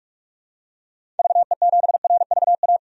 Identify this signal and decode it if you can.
Morse code, heard as a keyed tone, about 46 wpm, 700 Hz. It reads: VE7RVA
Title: DXCC-100 Call Signs - 6 Characters - 45wpm